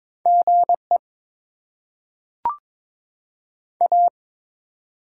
Morse code, heard as a keyed tone, about 22 wpm, 690 Hz. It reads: G E A
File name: Top 20 QSO Elements - 6x Character Spacing - 22wpm